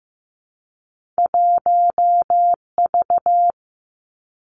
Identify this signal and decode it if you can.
Morse code, heard as a keyed tone, about 15 wpm, 700 Hz. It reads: 1V
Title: Two Challenging Letter-Number Combinations 15wpm